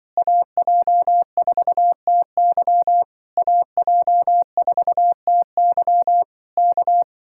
Morse code, heard as a keyed tone, about 24 wpm, 700 Hz. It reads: AJ4TY AJ4TY K
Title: Speed-Racing Common QSO Phrases Encore - 20wpm